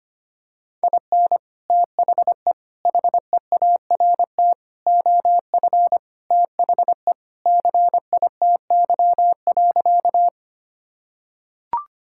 Morse code, heard as a keyed tone, about 25 wpm, 700 Hz. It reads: IN THE HEART OF THE CITY.
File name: practice / 25wpm / Sentences with 3-Word Phrases - 25wpm